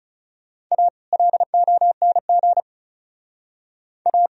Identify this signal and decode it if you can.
Morse code, heard as a keyed tone, about 37 wpm, 700 Hz. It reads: A LONG A
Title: Speed-Racing - Sets of 2 Words 30wpm